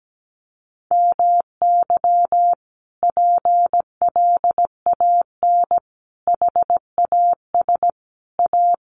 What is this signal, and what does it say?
Morse code, heard as a keyed tone, about 17 wpm, 700 Hz. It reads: MY PLAN HAS A
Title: Sentences from Top 1300 Words 17wpm